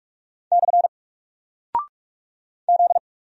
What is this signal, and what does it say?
Morse code, heard as a keyed tone, about 45 wpm, 700 Hz. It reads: / Z
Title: Single Character (Letter:Number:Punctuation:Prosign) 45wpm